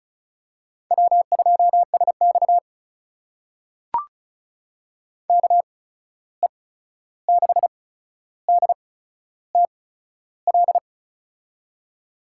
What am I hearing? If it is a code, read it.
Morse code, heard as a keyed tone, about 36 wpm, spaced as for 13 wpm, 700 Hz. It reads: W2SX KE6DTL
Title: USA Call Signs - 8x Character Spacing - 35wpm